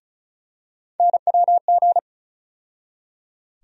Morse code, heard as a keyed tone, about 36 wpm, 700 Hz. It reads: NWG
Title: USA Call Sign Postfixes - 3 Letters 35wpm